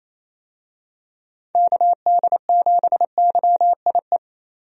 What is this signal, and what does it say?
Morse code, heard as a keyed tone, about 28 wpm, 700 Hz. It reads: KD7YIE
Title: USA Call Signs - 6 Characters - 28wpm